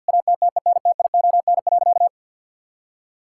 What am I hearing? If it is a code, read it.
Morse code, heard as a keyed tone, about 50 wpm, 700 Hz. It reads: ATTENTION.
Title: Sentences from Top 1000 Words - Review Part 6 - 50wpm